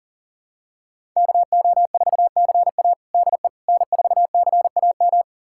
Morse code, heard as a keyed tone, about 40 wpm, 700 Hz. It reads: KO4CA DE N4CAM
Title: USA Callsign DE Callsign - 5 Characters - 40wpm